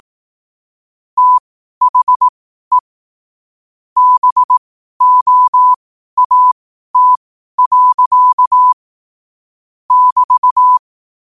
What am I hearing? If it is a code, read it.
Morse code, heard as a keyed tone, about 18 wpm, spaced as for 12 wpm, 1000 Hz. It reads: THE BOAT. <BT>
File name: 12wpm